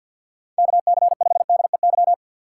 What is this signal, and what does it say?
Morse code, heard as a keyed tone, about 49 wpm, 700 Hz. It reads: KC5DEY
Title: USA Call Signs - 6 Characters - 6x Character Spacing - 50wpm